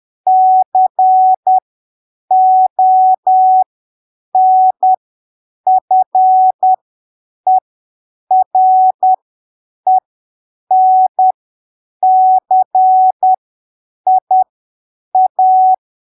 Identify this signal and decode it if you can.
Morse code, heard as a keyed tone, about 10 wpm, spaced as for 7 wpm, 750 Hz. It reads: CONFERENCIA
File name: arquivos / texto / texto 07 10ppm